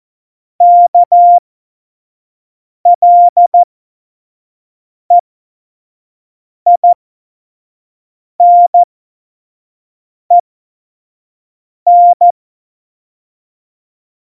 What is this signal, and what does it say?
Morse code, heard as a keyed tone, about 14 wpm, spaced as for 5 wpm, 700 Hz. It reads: KLEINEN
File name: Morsekurs / Witze / Witz 14 April 2020, 25 F70